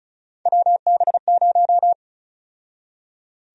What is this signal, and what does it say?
Morse code, heard as a keyed tone, about 34 wpm, 700 Hz. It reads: WB0
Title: USA Call Sign Prefixes - 35wpm